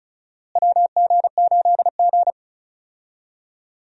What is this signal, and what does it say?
Morse code, heard as a keyed tone, about 36 wpm, 700 Hz. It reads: WG8G
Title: USA Call Signs - 4 Characters - Repeated 2x - 35wpm